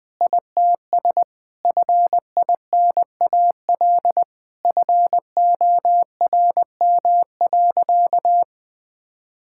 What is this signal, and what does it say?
Morse code, heard as a keyed tone, about 19 wpm, 700 Hz. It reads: ITS FINAL FORM.
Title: Sentences with 7-Word Phrases - 20wpm